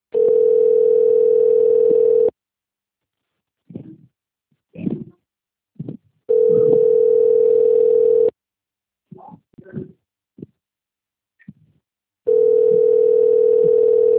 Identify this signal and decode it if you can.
Ringback tone (North America), 440 + 480 Hz.